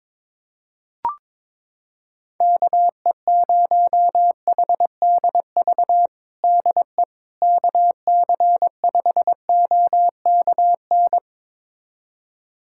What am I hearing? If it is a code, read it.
Morse code, heard as a keyed tone, about 22 wpm, 700 Hz. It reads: KE0HDV DE KC5OKN
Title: USA Callsign DE Callsign 22wpm